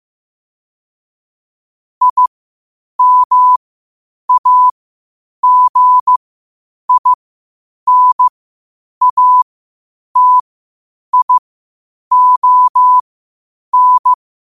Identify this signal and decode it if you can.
Morse code, heard as a keyed tone, about 15 wpm, spaced as for 8 wpm, 1000 Hz. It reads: IMAGINATION
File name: qotdfiles / 8wpm